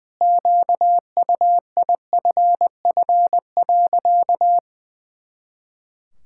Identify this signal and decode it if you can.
Morse code, heard as a keyed tone, about 20 wpm, 700 Hz. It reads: QUIFF.
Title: Pangram Challenge v2 - 20wpm